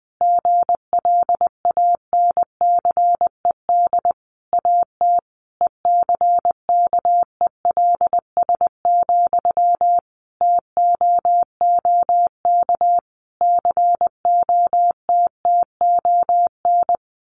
Morse code, heard as a keyed tone, about 20 wpm, 700 Hz. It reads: GLANCED AT ECKELS, TOOK COTTON
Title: A Sound Of Thunder 20wpm